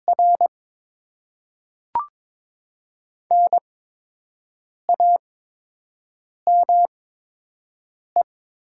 Morse code, heard as a keyed tone, about 22 wpm, spaced as for 6 wpm, 700 Hz. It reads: R NAME
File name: Top 20 QSO Elements - 8x Character Spacing - 22wpm